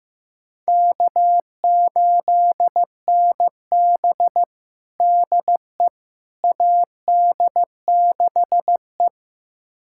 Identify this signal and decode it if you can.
Morse code, heard as a keyed tone, about 15 wpm, 700 Hz. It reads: K8NB DE AD6E
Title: USA Callsign DE Callsign - 4 Characters - 15wpm